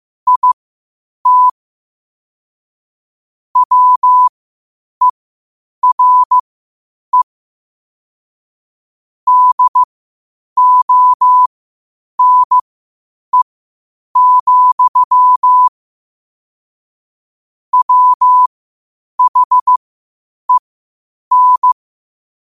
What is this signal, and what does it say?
Morse code, heard as a keyed tone, about 15 wpm, spaced as for 8 wpm, 1000 Hz. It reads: IT WERE DONE, WHEN
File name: qotdfiles / 8wpm